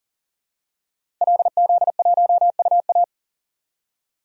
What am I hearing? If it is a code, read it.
Morse code, heard as a keyed tone, about 37 wpm, 700 Hz. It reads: LZ1UA
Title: DXCC-100 Call Signs - 5 Characters - 40wpm